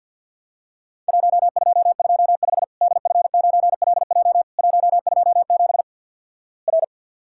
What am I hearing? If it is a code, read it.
Morse code, heard as a keyed tone, about 50 wpm, 700 Hz. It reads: 1225 DF9LJ 127 R
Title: Worked All Europe DX Contest - QTC Exchange with Repeats - 50wpm